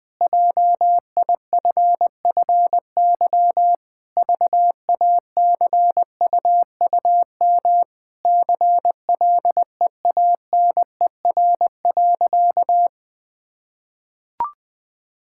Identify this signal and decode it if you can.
Morse code, heard as a keyed tone, about 20 wpm, 700 Hz. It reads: JIFFY VACUUM CLEANER.
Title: Pangram Challenge v2 - 20wpm